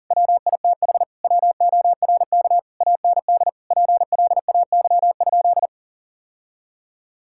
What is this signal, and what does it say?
Morse code, heard as a keyed tone, about 40 wpm, 690 Hz. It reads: WITH WORK AND PLAY?
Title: Sentences from Top 1000 Words - Review Part 6 - 40wpm